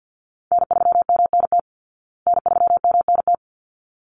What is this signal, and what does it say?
Morse code, heard as a keyed tone, about 50 wpm, 700 Hz. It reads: N3MNT N3MNT
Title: Call Signs Repeated 3x - 50wpm